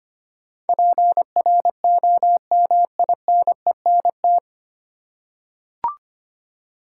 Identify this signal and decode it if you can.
Morse code, heard as a keyed tone, about 25 wpm, 700 Hz. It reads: PROMINENT
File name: Nine Letter Words - 25wpm